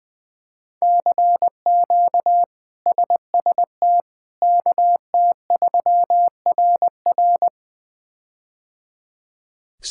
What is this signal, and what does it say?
Morse code, heard as a keyed tone, about 20 wpm, 700 Hz. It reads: CQ SST KT3RR
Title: K1USN SST QSO Practice - 20wpm